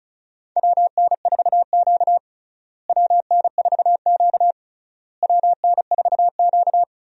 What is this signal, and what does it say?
Morse code, heard as a keyed tone, about 36 wpm, 700 Hz. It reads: WN4Q WN4Q WN4Q
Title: USA Call Signs - 4 Characters - Repeated 3x - 35wpm